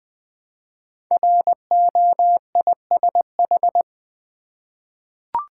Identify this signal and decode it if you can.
Morse code, heard as a keyed tone, about 20 wpm, 700 Hz. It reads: ROISH
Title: Code Groups - 5 Letters - 20wpm